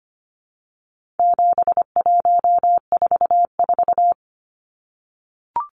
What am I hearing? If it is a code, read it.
Morse code, heard as a keyed tone, about 25 wpm, 700 Hz. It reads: 7144
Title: Numbers - 4-Digit 25wpm